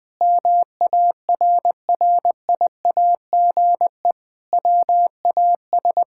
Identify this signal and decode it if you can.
Morse code, heard as a keyed tone, about 21 wpm, 700 Hz. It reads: MARRIAGE WAS
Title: Sentences from Top 1000 Words - Review Part 5 - 20wpm